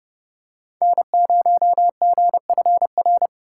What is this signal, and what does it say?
Morse code, heard as a keyed tone, about 30 wpm, 700 Hz. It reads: N0GFR
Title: USA Call Signs - 5 Characters - 2x Character Spacing - 30wpm